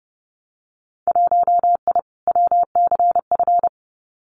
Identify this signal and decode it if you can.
Morse code, heard as a keyed tone, about 30 wpm, 700 Hz. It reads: 1I WCF
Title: Winter Field Day Exchanges Encore 30wpm